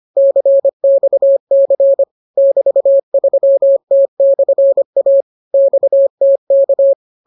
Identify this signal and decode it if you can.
Morse code, heard as a keyed tone, about 25 wpm, 550 Hz. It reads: CXC <BT>3T/A XTK